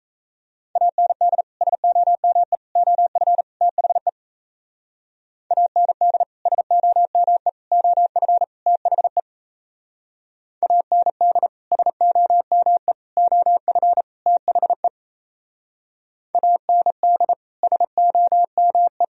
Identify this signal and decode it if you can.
Morse code, heard as a keyed tone, about 34 wpm, 700 Hz. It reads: AND SOME OF THE AND SOME OF THE AND SOME OF THE AND SOME
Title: Speed-Racing - Sets of 4 Words 28wpm